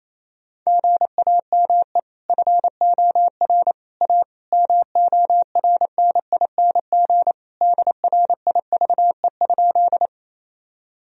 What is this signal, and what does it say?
Morse code, heard as a keyed tone, about 28 wpm, 700 Hz. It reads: GAME FOR A MORNING DRIVE?
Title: Sentences from Top 1000 Words - Review Part 3 - 28wpm